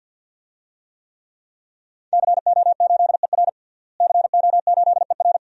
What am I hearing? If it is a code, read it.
Morse code, heard as a keyed tone, about 52 wpm, 700 Hz. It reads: CO8ER CO8ER
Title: DXCC-100 Call Signs - 5 Characters - Repeated 2x - 50wpm